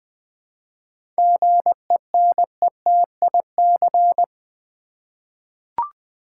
Morse code, heard as a keyed tone, about 20 wpm, 700 Hz. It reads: GENETIC
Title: 7 Letter Words - First 3 Letters Given - 20wpm